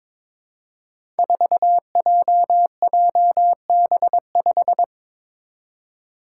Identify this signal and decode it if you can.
Morse code, heard as a keyed tone, about 22 wpm, 700 Hz. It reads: 4JJB5
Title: Code Groups - 5 Letter:Numbers - 22wpm